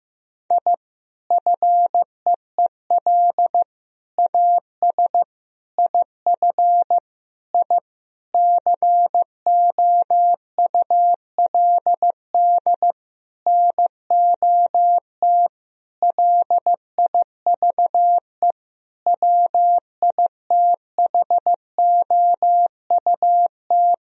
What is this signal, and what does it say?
Morse code, heard as a keyed tone, about 15 wpm, 700 Hz. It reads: I FEEL AS IF I COULD NOT LIVE WITHOUT
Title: Sentences from Top 1000 Words - Review Part 3 - 15wpm